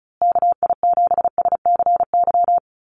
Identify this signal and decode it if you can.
Morse code, heard as a keyed tone, about 34 wpm, 700 Hz. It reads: KI7SCY
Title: USA Call Signs Repeated 3x 35wpm